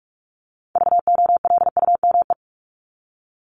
Morse code, heard as a keyed tone, about 44 wpm, 700 Hz. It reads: VOLUME
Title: Top 1751-2000 Words 45wpm